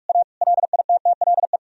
Morse code, heard as a keyed tone, about 46 wpm, 700 Hz. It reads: A LITTLE
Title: Sentences from Top 1000 Words - Review Part 1 - 45wpm